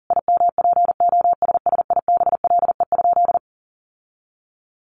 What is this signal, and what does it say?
Morse code, heard as a keyed tone, about 40 wpm, 700 Hz. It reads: IMPOSSIBLE?
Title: Sentences from Top 1100 Words 40wpm